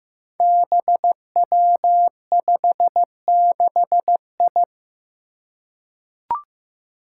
Morse code, heard as a keyed tone, about 15 wpm, 700 Hz. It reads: BW56I
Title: Five Challenging Letter-Number Combinations 15wpm